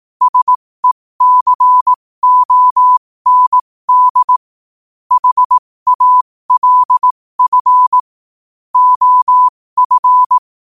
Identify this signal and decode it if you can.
Morse code, heard as a keyed tone, about 18 wpm, spaced as for 15 wpm, 1000 Hz. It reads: SECOND HALF OF
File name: qotdfiles / 15wpm